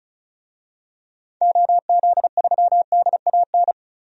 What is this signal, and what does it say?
Morse code, heard as a keyed tone, about 34 wpm, 700 Hz. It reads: OZ3DAN
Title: DXCC-100 Call Signs - 6 Characters - 35wpm